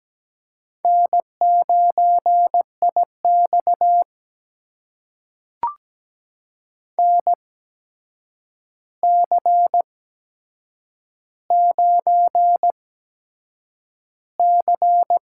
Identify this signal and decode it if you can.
Morse code, heard as a keyed tone, about 17 wpm, spaced as for 7 wpm, 700 Hz. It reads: N9IX NC9C
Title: USA Call Signs - 4 Characters - 8x Character Spacing - 17wpm